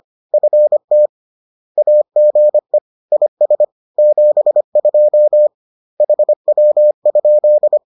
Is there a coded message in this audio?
Yes: Morse code from 0.3 to 7.8 s, heard as a keyed tone, about 25 wpm, 600 Hz. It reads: FT AGE IS 72 HW?